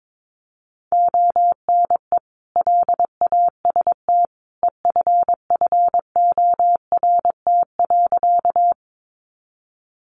Morse code, heard as a keyed tone, about 22 wpm, 700 Hz. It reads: ONE LAST EFFORT.
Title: Sentences from Top 1200 Words 22wpm